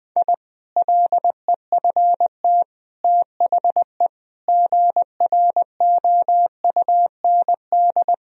Morse code, heard as a keyed tone, about 20 wpm, 700 Hz. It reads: I LEFT THE GROUND
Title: Sentences from Top 1000 Words - Review Part 2 - 20wpm